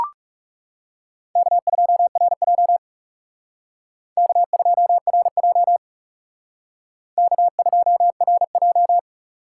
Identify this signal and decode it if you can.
Morse code, heard as a keyed tone, about 38 wpm, 700 Hz. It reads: K2RJ K2RJ K2RJ